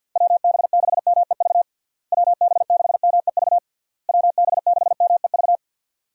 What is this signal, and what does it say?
Morse code, heard as a keyed tone, about 49 wpm, 700 Hz. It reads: WB6MEV WB6MEV WB6MEV
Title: USA Call Signs - 6 Characters - Repeated 3x - 50wpm